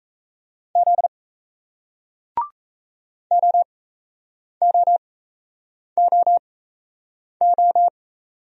Morse code, heard as a keyed tone, about 38 wpm, spaced as for 8 wpm, 700 Hz. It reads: Z OOOO